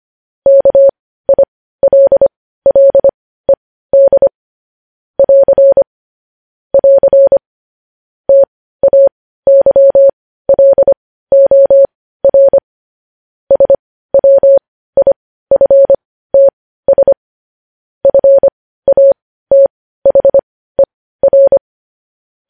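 Morse code, heard as a keyed tone, about 25 wpm, spaced as for 15 wpm, 550 Hz. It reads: KILLED <AR> <AR> TAYLOR SWIFTS FATHER
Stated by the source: Morse code news headlines sent at 15, preceded by vvv.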